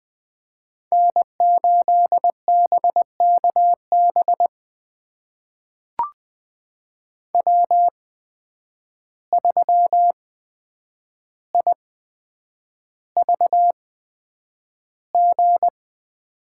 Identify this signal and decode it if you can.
Morse code, heard as a keyed tone, about 20 wpm, spaced as for 9 wpm, 700 Hz. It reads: N8BKB W3IVG
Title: USA Call Signs - 5 Characters - 8x Character Spacing - 20wpm